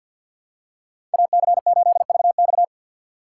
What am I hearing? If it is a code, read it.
Morse code, heard as a keyed tone, about 51 wpm, 690 Hz. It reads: AC8VX
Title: USA Call Signs - 5 Characters - 4x Character Spacing - 50wpm